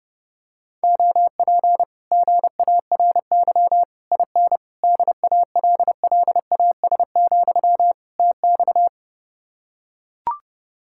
Morse code, heard as a keyed tone, about 30 wpm, 700 Hz. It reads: OP GARY IN DALLAS, TX